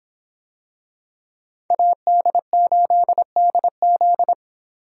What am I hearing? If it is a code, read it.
Morse code, heard as a keyed tone, about 24 wpm, 700 Hz. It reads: AD8DZ